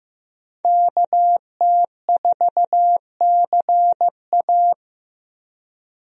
Morse code, heard as a keyed tone, about 15 wpm, 700 Hz. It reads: KT4CA
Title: USA Call Signs - 5 Characters - Repeated 2x - 15wpm